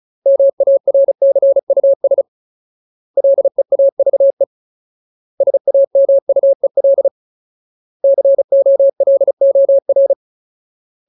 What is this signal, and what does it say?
Morse code, heard as a keyed tone, about 34 wpm, 550 Hz. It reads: MARCUS LEAVE SAMUEL COLOR
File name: Ultimate Word List / 35WPM Ultimate